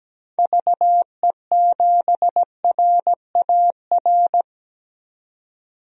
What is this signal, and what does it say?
Morse code, heard as a keyed tone, about 17 wpm, 700 Hz. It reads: VE7RAR
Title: DXCC-100 Call Signs - 6 Characters - 17wpm